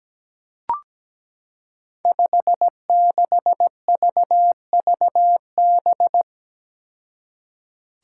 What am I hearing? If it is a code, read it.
Morse code, heard as a keyed tone, about 17 wpm, 700 Hz. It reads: 56VVB
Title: Five Challenging Letter-Number Combinations 17wpm